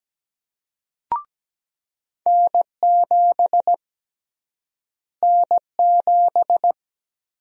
Morse code, heard as a keyed tone, about 17 wpm, 700 Hz. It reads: N7 N7
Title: USA Call Sign Prefixes - Repeated 2x - 17wpm